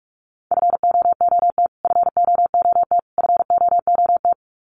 Morse code, heard as a keyed tone, about 46 wpm, 700 Hz. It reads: FOOT FOOT FOOT
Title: Top 500 Words Repeated 45wpm